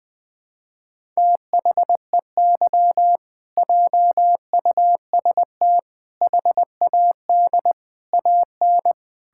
Morse code, heard as a keyed tone, about 20 wpm, 700 Hz. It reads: THEY JUST HAD AN
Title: Sentences from Top 1000 Words - Review Part 6 - 20wpm